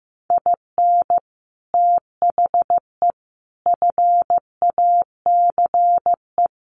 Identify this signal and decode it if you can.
Morse code, heard as a keyed tone, about 15 wpm, 700 Hz. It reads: IN THE FACE
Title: Sets of 5 Words Encore 15wpm